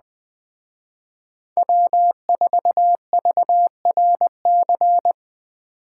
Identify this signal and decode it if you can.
Morse code, heard as a keyed tone, about 20 wpm, 700 Hz. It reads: W4VRC
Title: Speed Racing - USA 1x3 Call Signs 17wpm